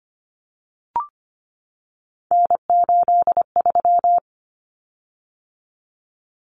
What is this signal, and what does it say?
Morse code, heard as a keyed tone, about 25 wpm, 700 Hz. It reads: N83
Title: Common Cut Numbers - 3-Digit 25wpm